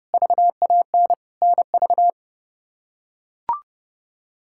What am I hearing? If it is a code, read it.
Morse code, heard as a keyed tone, about 32 wpm, 700 Hz. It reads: VAN NV
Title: K1USN SST QSO Practice - 30wpm